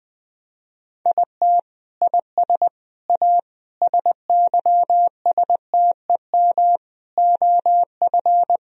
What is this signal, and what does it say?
Morse code, heard as a keyed tone, about 20 wpm, 700 Hz. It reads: IT IS A SYSTEM OF
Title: Sentences from Top 1000 Words - Review Part 4 - 20wpm